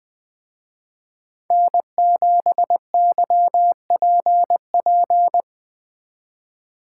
Morse code, heard as a keyed tone, about 19 wpm, 700 Hz. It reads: N7YPP
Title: Speed Racing - USA 1x3 Call Signs 17wpm